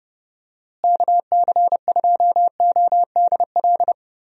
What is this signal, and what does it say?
Morse code, heard as a keyed tone, about 30 wpm, 700 Hz. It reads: KC2ODL
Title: USA Call Signs - 6 Characters - Repeated 2x - 30wpm